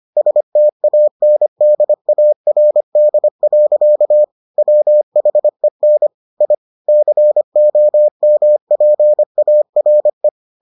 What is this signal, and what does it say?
Morse code, heard as a keyed tone, about 25 wpm, 600 Hz. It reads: STANDARD. WHEN I COMPARE